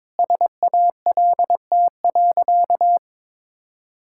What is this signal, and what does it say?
Morse code, heard as a keyed tone, about 22 wpm, 700 Hz. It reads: SALT.
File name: practice / 22wpm / Sentences with 8-Word Phrases - 22wpm